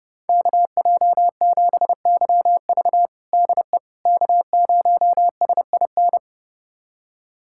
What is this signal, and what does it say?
Morse code, heard as a keyed tone, about 30 wpm, 700 Hz. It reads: KJ7YV DE K0SIN
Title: USA Callsign DE Callsign - 5 Characters - 30wpm